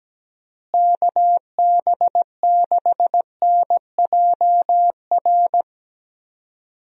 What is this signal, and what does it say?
Morse code, heard as a keyed tone, about 17 wpm, 700 Hz. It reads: KB6NJR
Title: USA Call Signs - 6 Characters - Repeated 3x - 17wpm